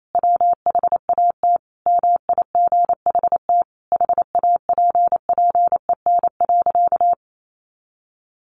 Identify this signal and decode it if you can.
Morse code, heard as a keyed tone, about 28 wpm, 700 Hz. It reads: WHAT MIGHT HAPPEN.
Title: Sentences from Top 400 Words 28wpm